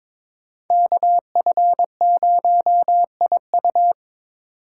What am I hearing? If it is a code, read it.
Morse code, heard as a keyed tone, about 21 wpm, 700 Hz. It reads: KF0IU
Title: USA Call Signs - 5 Characters - 2x Character Spacing - 22wpm